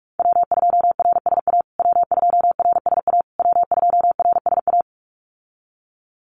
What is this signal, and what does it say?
Morse code, heard as a keyed tone, about 44 wpm, 700 Hz. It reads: W2RSA W2RSA W2RSA
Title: USA Call Signs Repeated 3x 45wpm